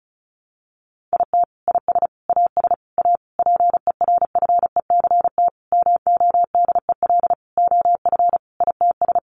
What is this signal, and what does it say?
Morse code, heard as a keyed tone, about 34 wpm, 700 Hz. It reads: IT IS AS A PERFECT MODEL OF ITS
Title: Sentences from Top 1000 Words 35wpm